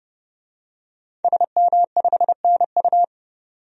Morse code, heard as a keyed tone, about 30 wpm, 700 Hz. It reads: SM5NU
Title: DXCC-100 Call Signs - 5 Characters - 30wpm